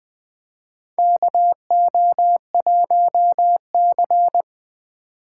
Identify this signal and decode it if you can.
Morse code, heard as a keyed tone, about 20 wpm, 700 Hz. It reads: KO1C
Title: USA Call Signs - 4 Characters - 2x Character Spacing - 20wpm